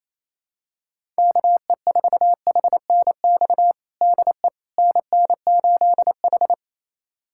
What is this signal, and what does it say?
Morse code, heard as a keyed tone, about 28 wpm, 700 Hz. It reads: KE4HNX DE NN8H
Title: USA Callsign DE Callsign 28wpm